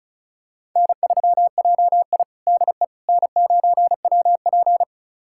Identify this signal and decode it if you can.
Morse code, heard as a keyed tone, about 36 wpm, 700 Hz. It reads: N3JI DE N9WP
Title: USA Callsign DE Callsign - 4 Characters - 35wpm